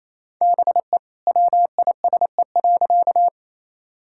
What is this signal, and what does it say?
Morse code, heard as a keyed tone, about 27 wpm, 700 Hz. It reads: BE WISE.
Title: Sentences from Top 1600 Words 28wpm